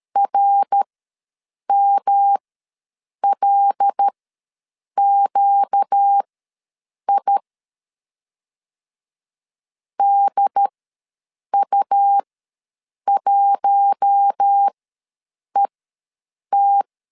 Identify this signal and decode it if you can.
Morse code, heard as a keyed tone, about 13 wpm, spaced as for 7 wpm, 800 Hz. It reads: RMLQI DU1ET
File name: LEZIONI / 5 LBVC / 5 L ws 12wpm 2958